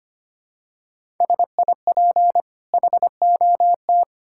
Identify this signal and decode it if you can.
Morse code, heard as a keyed tone, about 25 wpm, 700 Hz. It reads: SIP HOT
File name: Easy Sentences with Hints - 25wpm